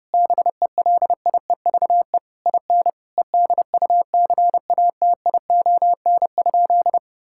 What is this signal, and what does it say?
Morse code, heard as a keyed tone, about 30 wpm, 700 Hz. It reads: BELIEVE IN EDUCATION?
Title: Sentences from Top 1000 Words - Review Part 6 - 30wpm